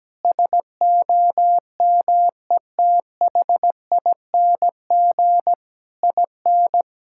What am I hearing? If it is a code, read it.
Morse code, heard as a keyed tone, about 17 wpm, 700 Hz. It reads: SOMETHING IN
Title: Sentences from Top 1000 Words - Review Part 5 - 17wpm